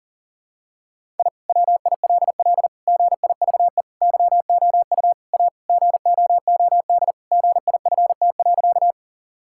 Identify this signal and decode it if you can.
Morse code, heard as a keyed tone, about 40 wpm, 700 Hz. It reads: I WILL GIVE YOU A GOOD GIFT.
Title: Sentences from Top 1000 Words - Review Part 6 - 40wpm